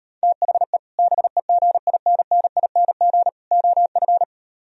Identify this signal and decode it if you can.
Morse code, heard as a keyed tone, about 38 wpm, 700 Hz. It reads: THE BEGINNING OF